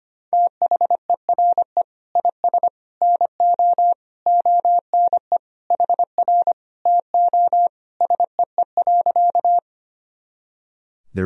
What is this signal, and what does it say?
Morse code, heard as a keyed tone, about 25 wpm, 700 Hz. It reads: THERE IS NO ONE HR TO SEE.
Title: Sentences from Top 200 Words and Abbreviations 25wpm